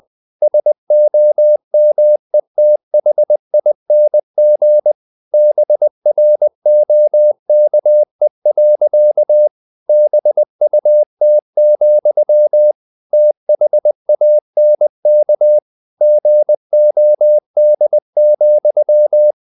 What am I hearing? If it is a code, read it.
Morse code, heard as a keyed tone, about 20 wpm, 600 Hz. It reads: SOMETHING BROKE. BUT, THANK GOD,
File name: the old man and the sea / the old man and the sea part 1 20wpm